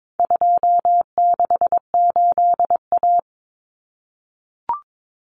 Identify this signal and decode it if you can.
Morse code, heard as a keyed tone, about 22 wpm, 700 Hz. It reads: 268A
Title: Common Cut Numbers - 4-Digit 22wpm